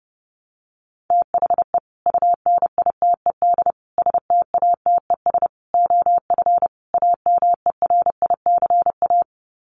Morse code, heard as a keyed tone, about 28 wpm, 700 Hz. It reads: THE UNITED STATES OF AMERICA
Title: Sets of 5 Words 30wpm